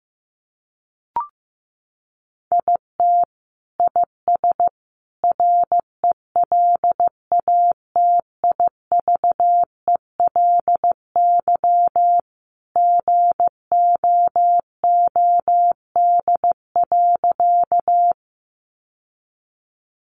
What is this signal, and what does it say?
Morse code, heard as a keyed tone, about 15 wpm, 700 Hz. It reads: IT IS RELATIVELY GOOD.
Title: Sentences from Top 1100 Words 15wpm